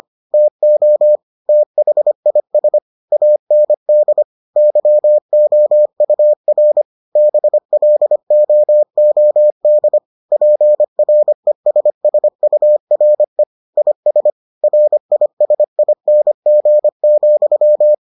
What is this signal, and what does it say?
Morse code, heard as a keyed tone, about 25 wpm, 600 Hz. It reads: TO THIS AND YOUR BLOOD PRESSURE IS RISING,